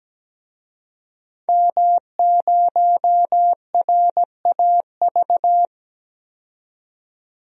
Morse code, heard as a keyed tone, about 17 wpm, 700 Hz. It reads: M0RAV
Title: DXCC-100 Call Signs - 5 Characters - 17wpm